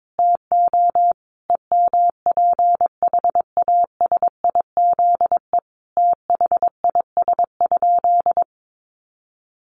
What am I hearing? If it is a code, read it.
Morse code, heard as a keyed tone, about 22 wpm, 700 Hz. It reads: TO EMPHASIZE THIS?
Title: Sentences from Top 1500 Words 22wpm